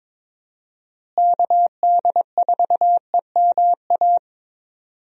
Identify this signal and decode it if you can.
Morse code, heard as a keyed tone, about 21 wpm, 700 Hz. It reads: KD4EMA
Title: USA Call Signs - 6 Characters - Repeated 2x - 22wpm